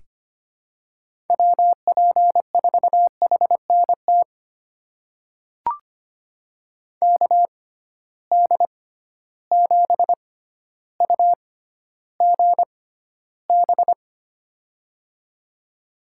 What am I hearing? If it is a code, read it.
Morse code, heard as a keyed tone, about 25 wpm, spaced as for 13 wpm, 700 Hz. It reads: WP4HNT KD7UGB
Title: USA Call Signs - 6 Characters - 6x Character Spacing - 25wpm